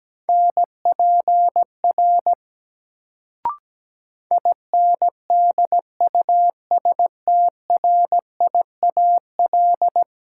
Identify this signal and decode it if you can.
Morse code, heard as a keyed tone, about 17 wpm, 700 Hz. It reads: NPR INDUSTRIAL
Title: Sets of 5 Words - New Word Drill - 17wpm